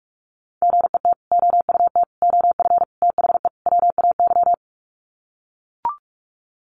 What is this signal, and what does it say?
Morse code, heard as a keyed tone, about 44 wpm, 700 Hz. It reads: GET OUT OF THE WAY
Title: Sets of 5 Words Encore 45wpm